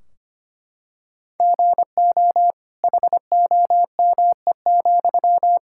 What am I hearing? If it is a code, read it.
Morse code, heard as a keyed tone, about 25 wpm, 700 Hz. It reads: GO HOME,
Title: Sentences from Top 1600 Words 25wpm